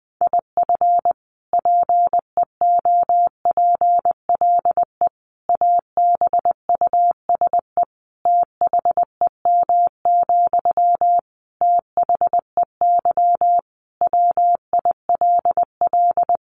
Morse code, heard as a keyed tone, about 20 wpm, 700 Hz. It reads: IF PEOPLE ABUSE THEM, THEY WILL
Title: Sentences from Top 1300 Words 20wpm